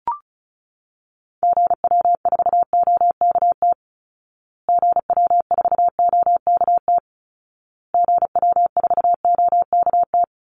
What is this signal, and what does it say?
Morse code, heard as a keyed tone, about 34 wpm, 700 Hz. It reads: GW4OKT GW4OKT GW4OKT
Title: Call Signs Repeated 2x - 35wpm